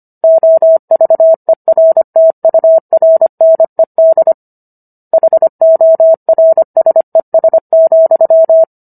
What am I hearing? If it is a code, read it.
Morse code, heard as a keyed tone, about 25 wpm, 650 Hz. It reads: OVERTURNED HORSES,
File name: Book One / 16-Chapter 16 - 25 WPM